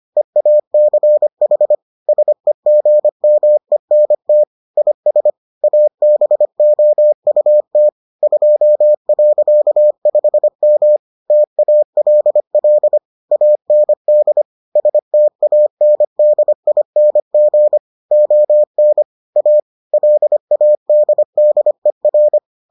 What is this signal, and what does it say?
Morse code, heard as a keyed tone, about 25 wpm, 600 Hz. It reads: EACH SEGMENT IS ABOUT 2.5M TALL AND STANDING ON A LADDER